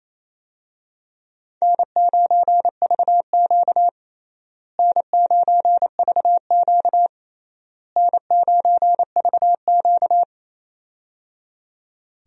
Morse code, heard as a keyed tone, about 28 wpm, 700 Hz. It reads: N9VQ N9VQ N9VQ
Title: USA Call Signs - 4 Characters - Repeated 3x - 28wpm